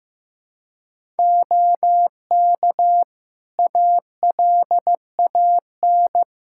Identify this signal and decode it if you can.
Morse code, heard as a keyed tone, about 15 wpm, 700 Hz. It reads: OK ALAN
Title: Practice QSO - Exchange 3 - More Info - 15wpm